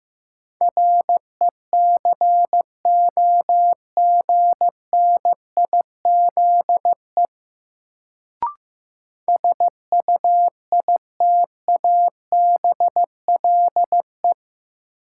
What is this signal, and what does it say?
Morse code, heard as a keyed tone, about 15 wpm, 700 Hz. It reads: RECOGNIZE SUITABLE
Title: Sentences from Top 1400 Words - New Word Drill - 15wpm